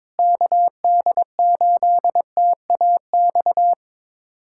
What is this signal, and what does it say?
Morse code, heard as a keyed tone, about 21 wpm, 700 Hz. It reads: KD8TAX
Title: USA Callsign DE Callsign 22wpm